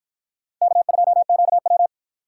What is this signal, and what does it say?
Morse code, heard as a keyed tone, about 52 wpm, 700 Hz. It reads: K2QW